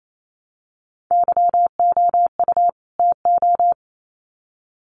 Morse code, heard as a keyed tone, about 28 wpm, 700 Hz. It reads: YOU TO
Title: Sets of 2 Words 28wpm